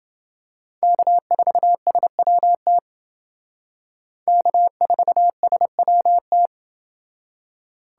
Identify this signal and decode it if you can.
Morse code, heard as a keyed tone, about 30 wpm, 700 Hz. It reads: K4SWT K4SWT